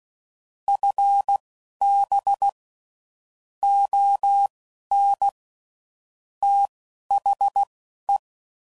Morse code, heard as a keyed tone, about 16 wpm, spaced as for 11 wpm, 800 Hz. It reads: FB ON THE
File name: CW Code Groups @15WPM/QSOs/WD0FBT Spaced